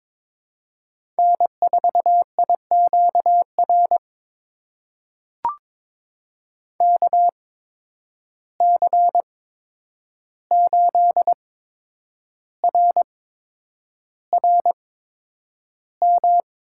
Morse code, heard as a keyed tone, about 22 wpm, spaced as for 9 wpm, 700 Hz. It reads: N4IQR KC8RRM
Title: USA Call Signs - 8x Character Spacing - 22wpm